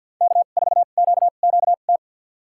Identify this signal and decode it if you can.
Morse code, heard as a keyed tone, about 52 wpm, spaced as for 36 wpm, 700 Hz. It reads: K4QQT
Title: USA Call Signs - 5 Characters - 2x Character Spacing - 50wpm